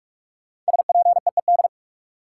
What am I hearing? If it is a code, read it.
Morse code, heard as a keyed tone, about 42 wpm, 700 Hz. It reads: SPEED
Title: Sentences from Top 800 Words - New Word Drill - 45wpm